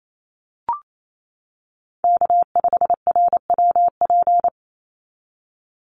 Morse code, heard as a keyed tone, about 28 wpm, 700 Hz. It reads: K5RWP
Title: NA SOTA Chaser Callsigns 28wpm